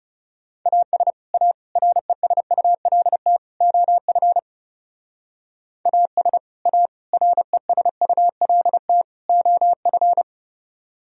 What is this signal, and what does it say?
Morse code, heard as a keyed tone, about 37 wpm, 700 Hz. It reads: AS A RESULT OF AS A RESULT OF
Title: Speed-Racing - Sets of 4 Words 30wpm